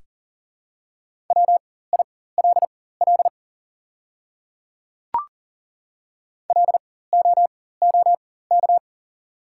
Morse code, heard as a keyed tone, about 40 wpm, spaced as for 19 wpm, 690 Hz. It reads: WILL LOOK
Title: 4x Character Spacing - Top 100 Words - 40wpm